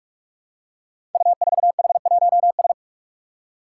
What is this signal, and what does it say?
Morse code, heard as a keyed tone, about 46 wpm, 700 Hz. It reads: U4H1S
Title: Five Challenging Letter-Number Combinations 45wpm